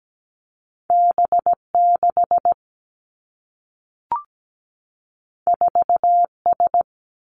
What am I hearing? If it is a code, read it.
Morse code, heard as a keyed tone, about 17 wpm, 700 Hz. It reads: B 6 4 S
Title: Two Challenging Letter-Number Combinations 17wpm